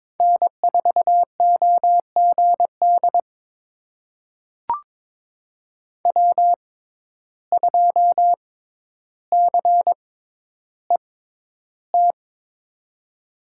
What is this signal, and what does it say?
Morse code, heard as a keyed tone, about 23 wpm, spaced as for 11 wpm, 700 Hz. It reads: N4OGD W2CET
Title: USA Call Signs - 5 Characters - 6x Character Spacing - 22wpm